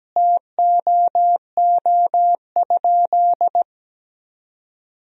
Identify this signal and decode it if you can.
Morse code, heard as a keyed tone, about 17 wpm, 700 Hz. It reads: TOO?
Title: 3x Word-Spacing - Sentences from Top 200 Words - 17wpm